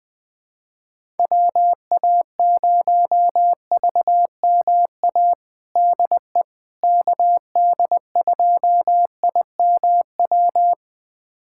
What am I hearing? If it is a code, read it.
Morse code, heard as a keyed tone, about 20 wpm, 700 Hz. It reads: WA0VMA DE KD2IMW
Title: USA Callsign DE Callsign 20wpm